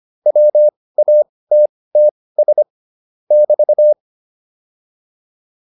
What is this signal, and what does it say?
Morse code, heard as a keyed tone, about 25 wpm, spaced as for 18 wpm, 600 Hz. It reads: WATTS <BT>